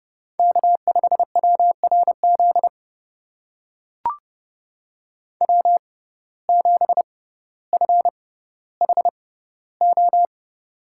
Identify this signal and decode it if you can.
Morse code, heard as a keyed tone, about 32 wpm, spaced as for 16 wpm, 700 Hz. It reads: K5WRZ W7FHO
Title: USA Call Signs - 5 Characters - 6x Character Spacing - 30wpm